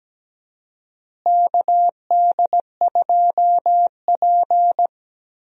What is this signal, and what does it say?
Morse code, heard as a keyed tone, about 17 wpm, 700 Hz. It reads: KD2P
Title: USA Call Signs - 4 Characters - Repeated 3x - 17wpm